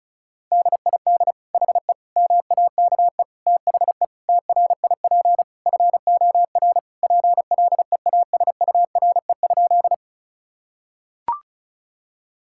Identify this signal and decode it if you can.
Morse code, heard as a keyed tone, about 34 wpm, 700 Hz. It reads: DID HE MAKE THE TRIP FOR PLEASURE?
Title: Sentences from Top 1600 Words 35wpm